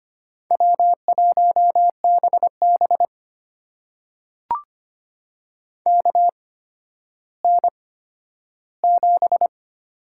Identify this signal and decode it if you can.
Morse code, heard as a keyed tone, about 25 wpm, spaced as for 12 wpm, 700 Hz. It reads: W1BB KN7
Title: USA Call Signs - 4 Characters - 8x Character Spacing - 25wpm